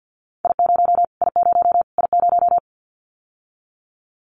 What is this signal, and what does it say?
Morse code, heard as a keyed tone, about 50 wpm, 700 Hz. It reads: I0 I0 I0
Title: Callsign Prefixes for Top 25 DXCC - Repeated 3x - 50wpm